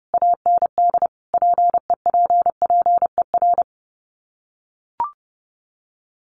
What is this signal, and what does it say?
Morse code, heard as a keyed tone, about 30 wpm, 700 Hz. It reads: AND PEPPER
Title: Sets of 5 Words 30wpm